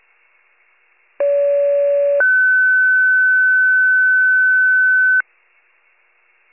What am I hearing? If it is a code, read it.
Two-tone page, heard as a one-second tone, then a three-second tone, about 570 then 1500 Hz. It is heard from 1.2 to 5.2 s.